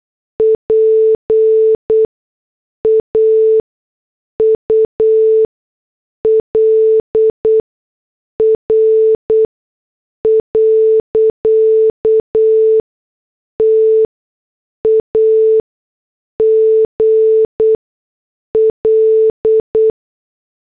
Morse code, heard as a keyed tone, about 8 wpm, spaced as for 6 wpm, 440 Hz. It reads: PAUL R. TAGL
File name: morse / headlines